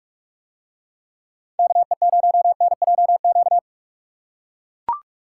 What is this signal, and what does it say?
Morse code, heard as a keyed tone, about 40 wpm, 700 Hz. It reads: KE0NJQ